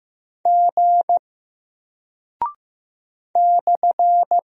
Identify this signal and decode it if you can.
Morse code, heard as a keyed tone, about 15 wpm, 700 Hz. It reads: G /
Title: Single Character (Letter:Number:Punctuation:Prosign) 15wpm